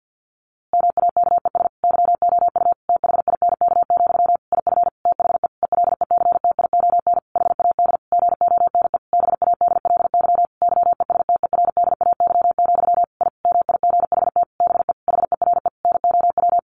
Morse code, heard as a keyed tone, about 50 wpm, 700 Hz. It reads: MAKES YOU THINK, IF THE ELECTION HAD GONE BADLY YESTERDAY, I MIGHT BE HERE NOW
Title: A Sound Of Thunder 50wpm